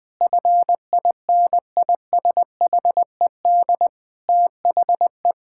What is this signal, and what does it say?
Morse code, heard as a keyed tone, about 19 wpm, 700 Hz. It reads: FINISHED THE
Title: Sentences with 5-Word Phrases - 20wpm